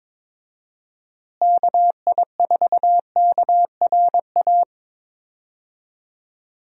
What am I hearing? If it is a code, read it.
Morse code, heard as a keyed tone, about 22 wpm, 700 Hz. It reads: KI4KRA
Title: USA Call Signs - 6 Characters - Repeated 3x - 22wpm